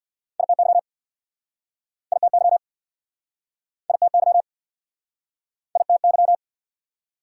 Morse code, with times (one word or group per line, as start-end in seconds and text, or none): 0.4-0.8 ITY
2.1-2.6 ITY
3.9-4.4 ITY
5.8-6.3 ITY